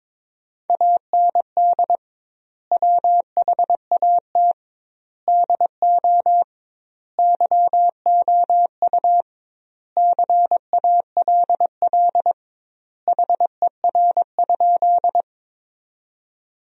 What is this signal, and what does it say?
Morse code, heard as a keyed tone, about 22 wpm, 700 Hz. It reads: AND WHAT DO YOU CALL HER?
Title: 2x Word-Spacing - Sentences from Top 100 Words - 22wpm